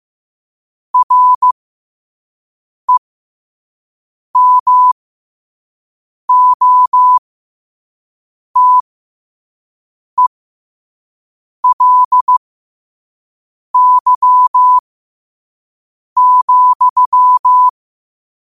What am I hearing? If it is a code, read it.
Morse code, heard as a keyed tone, about 15 wpm, spaced as for 5 wpm, 1000 Hz. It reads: REMOTELY,
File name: qotdfiles / 5wpm